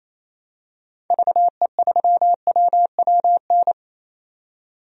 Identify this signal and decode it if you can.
Morse code, heard as a keyed tone, about 28 wpm, 700 Hz. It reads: VE3WWN
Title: DXCC-100 Call Signs - 6 Characters - 28wpm